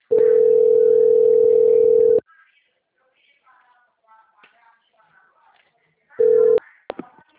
Ringback tone (North America), 440 + 480 Hz.